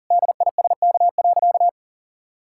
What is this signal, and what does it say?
Morse code, heard as a keyed tone, about 40 wpm, 700 Hz. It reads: DISK.
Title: Sentences from Top 1000 Words - Review Part 6 - 40wpm